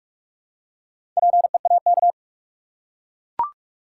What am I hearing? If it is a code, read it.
Morse code, heard as a keyed tone, about 44 wpm, 700 Hz. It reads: PEAK
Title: Four Letter Words - 45wpm